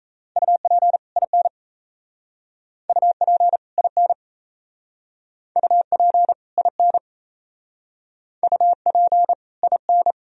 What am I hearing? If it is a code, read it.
Morse code, heard as a keyed tone, about 35 wpm, 700 Hz. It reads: UP IN UP IN UP IN UP IN